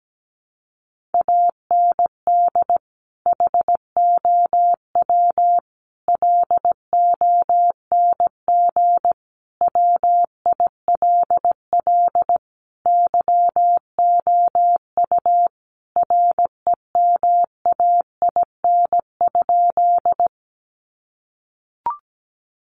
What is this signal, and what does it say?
Morse code, heard as a keyed tone, about 17 wpm, 700 Hz. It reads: AND HOW LONG WILL YOU REMAIN?
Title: Sentences from Top 1200 Words 17wpm